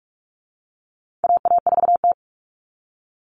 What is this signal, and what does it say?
Morse code, heard as a keyed tone, about 44 wpm, 700 Hz. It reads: AA4T
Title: All Cut Numbers - 4-Digit 45wpm